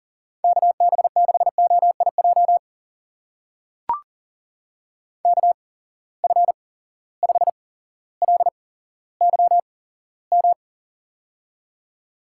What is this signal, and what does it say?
Morse code, heard as a keyed tone, about 41 wpm, spaced as for 17 wpm, 700 Hz. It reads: KB6OIJ KF5LYM
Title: USA Call Signs - 6 Characters - 8x Character Spacing - 40wpm